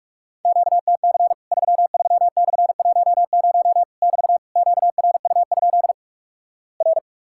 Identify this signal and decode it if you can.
Morse code, heard as a keyed tone, about 46 wpm, 700 Hz. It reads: QTC 33/10 <BT> QRU? R
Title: Worked All Europe DX Contest - QTC Exchange - 45wpm